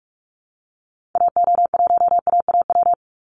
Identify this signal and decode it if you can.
Morse code, heard as a keyed tone, about 44 wpm, 700 Hz. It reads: AO1AAW
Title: Call Signs - 45wpm